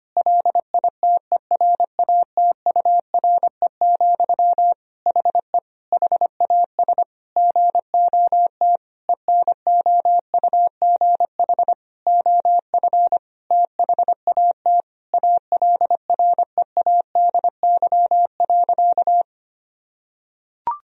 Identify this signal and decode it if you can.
Morse code, heard as a keyed tone, about 25 wpm, 700 Hz. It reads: LITERATURE, HE HAS GOT ENOUGH OF THAT ALREADY.
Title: Sentences from Top 1000 Words - Review Part 3 - 25wpm